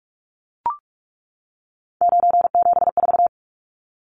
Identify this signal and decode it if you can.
Morse code, heard as a keyed tone, about 44 wpm, 700 Hz. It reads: 974
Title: Numbers - 3-Digit 45wpm